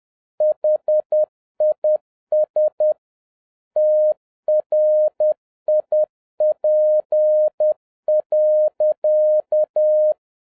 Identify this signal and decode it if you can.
Morse code, heard as a keyed tone, about 10 wpm, 610 Hz. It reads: HIS TRIP.
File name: 2011-07-mp3 / n46 JOHN K3TN 10wpm